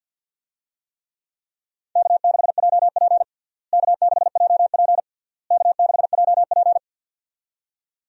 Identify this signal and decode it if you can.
Morse code, heard as a keyed tone, about 50 wpm, 700 Hz. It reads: K6JP K6JP K6JP
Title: USA Call Signs - 4 Characters - Repeated 3x - 50wpm